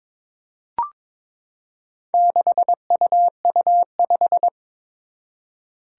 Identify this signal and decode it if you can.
Morse code, heard as a keyed tone, about 22 wpm, 700 Hz. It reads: 6UU5
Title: Four Challenging Letter-Number Combinations 22wpm